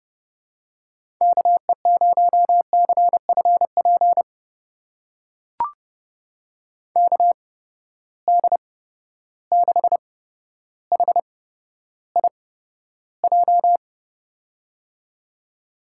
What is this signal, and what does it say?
Morse code, heard as a keyed tone, about 32 wpm, spaced as for 13 wpm, 700 Hz. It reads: KE0CFP KD6HIJ
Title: USA Call Signs - 8x Character Spacing - 30wpm